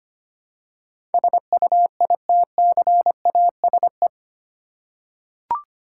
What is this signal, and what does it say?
Morse code, heard as a keyed tone, about 25 wpm, 700 Hz. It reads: SUITCASE
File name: practice / 25wpm / Easy Words with Hints - 25wpm